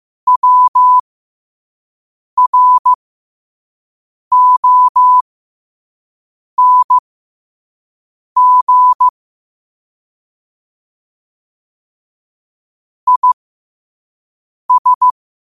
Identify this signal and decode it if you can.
Morse code, heard as a keyed tone, about 15 wpm, spaced as for 5 wpm, 1000 Hz. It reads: WRONG IS
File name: qotdfiles / 5wpm